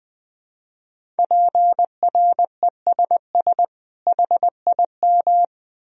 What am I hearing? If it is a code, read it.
Morse code, heard as a keyed tone, about 20 wpm, 700 Hz. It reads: PRESS HIM
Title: Sentences from Top 1000 Words - Review Part 1 - 20wpm